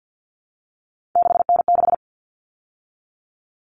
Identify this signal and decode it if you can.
Morse code, heard as a keyed tone, about 50 wpm, 700 Hz. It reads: 6N6
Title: All Cut Numbers - 3-Digit 50wpm